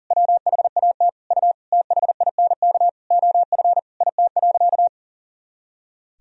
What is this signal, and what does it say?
Morse code, heard as a keyed tone, about 40 wpm, 700 Hz. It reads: WHAT U THINK OF IT.
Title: Sentences from Top 200 Words and Abbreviations 40wpm